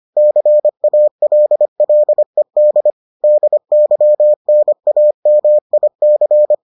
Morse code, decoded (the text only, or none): CALLED DYNAMIC